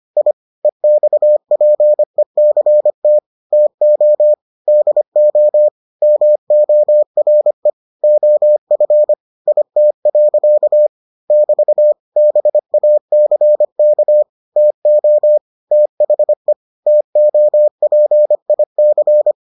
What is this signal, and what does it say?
Morse code, heard as a keyed tone, about 25 wpm, 600 Hz. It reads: I EXPECT TO DO MORE OF IT. <BT> BACK TO THE TOPIC